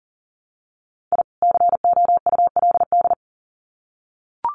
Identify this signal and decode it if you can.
Morse code, heard as a keyed tone, about 40 wpm, 700 Hz. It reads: I COULD
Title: Sets of 2 Words 40wpm